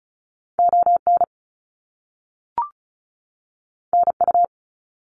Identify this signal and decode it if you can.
Morse code, heard as a keyed tone, about 36 wpm, 700 Hz. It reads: ON NU
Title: Canadian Provinces and Territories 35wpm